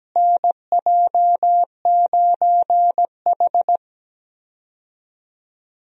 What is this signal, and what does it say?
Morse code, heard as a keyed tone, about 17 wpm, 700 Hz. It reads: NJ9H
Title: USA Call Signs - 4 Characters - Repeated 3x - 17wpm